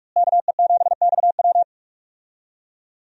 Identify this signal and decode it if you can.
Morse code, heard as a keyed tone, about 45 wpm, 700 Hz. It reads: KE7XW
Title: USA Call Signs - 5 Characters - Repeated 2x - 45wpm